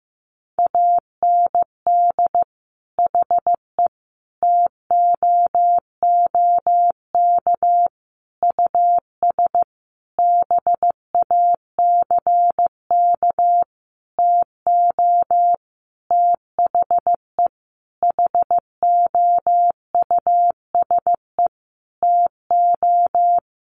Morse code, heard as a keyed tone, about 15 wpm, 700 Hz. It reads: AND HE TOOK US BACK TO THE HOUSE TO
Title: Sentences from Top 1500 Words 15wpm